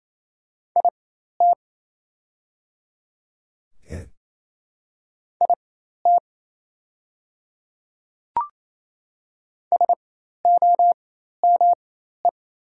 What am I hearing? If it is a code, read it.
Morse code, heard as a keyed tone, about 27 wpm, spaced as for 13 wpm, 700 Hz. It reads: IT IT SOME
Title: 4x Character Spacing - Top 500 Words - 28wpm